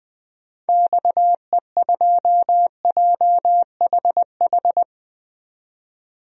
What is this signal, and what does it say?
Morse code, heard as a keyed tone, about 20 wpm, 700 Hz. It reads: XE2JHH
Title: DXCC-100 Call Signs - 6 Characters - 20wpm